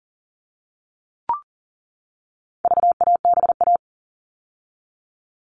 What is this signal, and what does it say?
Morse code, heard as a keyed tone, about 40 wpm, 700 Hz. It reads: VABA
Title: All Cut Numbers - 4-Digit 40wpm